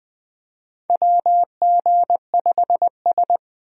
Morse code, heard as a keyed tone, about 20 wpm, 700 Hz. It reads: WG5S
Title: USA Call Signs - 4 Characters - Repeated 2x - 20wpm